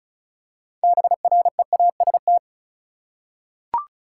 Morse code, heard as a keyed tone, about 36 wpm, 700 Hz. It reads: BREAST
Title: Six Letter Words - 35wpm